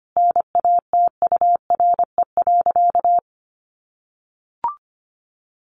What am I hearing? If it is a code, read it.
Morse code, heard as a keyed tone, about 25 wpm, 700 Hz. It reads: NATURE.
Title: Sentences from Top 600 Words 25wpm